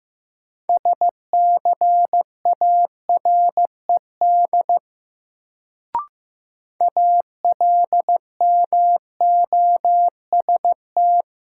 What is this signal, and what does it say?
Morse code, heard as a keyed tone, about 15 wpm, 700 Hz. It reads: SCARED ALMOST
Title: Sentences from Top 700 Words - New Word Drill - 15wpm